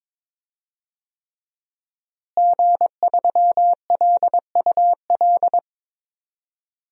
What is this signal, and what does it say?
Morse code, heard as a keyed tone, about 21 wpm, 700 Hz. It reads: G3LUL
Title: DXCC-100 Call Signs - 5 Characters - Repeated 3x - 22wpm